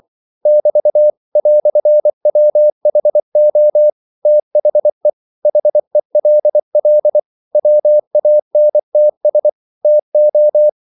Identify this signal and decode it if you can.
Morse code, heard as a keyed tone, about 24 wpm, 600 Hz. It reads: <BT> "WHO THE HELL WANTS TO
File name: CW / Practice / EnglishQuotations0424